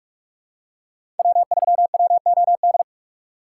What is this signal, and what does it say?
Morse code, heard as a keyed tone, about 43 wpm, 700 Hz. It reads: W3WOD
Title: USA Call Signs - 5 Characters - 6x Character Spacing - 45wpm